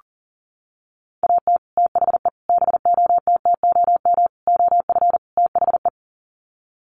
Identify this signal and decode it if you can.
Morse code, heard as a keyed tone, about 40 wpm, 700 Hz. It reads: AT THE BOTTOM OF THE
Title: Sets of 5 Words 40wpm